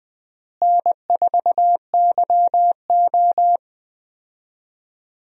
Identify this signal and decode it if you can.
Morse code, heard as a keyed tone, about 20 wpm, 700 Hz. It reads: N4YO
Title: USA Call Signs - 4 Characters - Repeated 3x - 20wpm